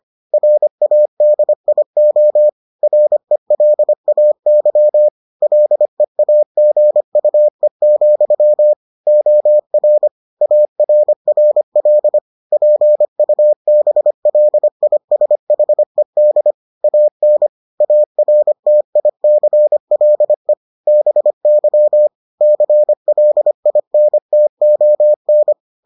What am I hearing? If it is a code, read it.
Morse code, heard as a keyed tone, about 25 wpm, 600 Hz. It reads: RADIO RELAY LEAGUE, OR ARRL PUBLISHED AN ARTICLE BY CLINTON